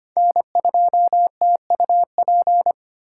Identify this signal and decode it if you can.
Morse code, heard as a keyed tone, about 32 wpm, 700 Hz. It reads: N2TUP